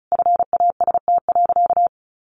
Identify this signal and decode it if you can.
Morse code, heard as a keyed tone, about 34 wpm, 700 Hz. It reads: FAST.
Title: Sentences from Top 1100 Words 35wpm